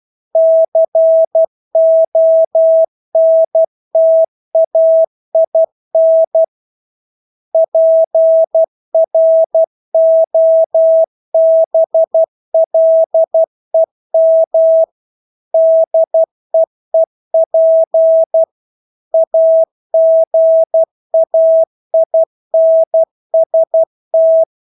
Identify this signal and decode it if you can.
Morse code, heard as a keyed tone, about 12 wpm, 650 Hz. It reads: CONTAIN PROBLEM DEEP AGAINST
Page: Engelske ord 12 wpm